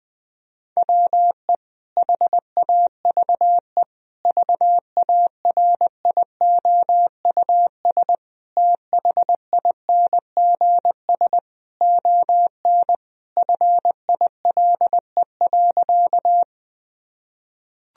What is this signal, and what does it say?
Morse code, heard as a keyed tone, about 21 wpm, 700 Hz. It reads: WE HAVE VARIOUS THINGS ON FILE.
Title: Sentences from Top 1000 Words - Review Part 4 - 20wpm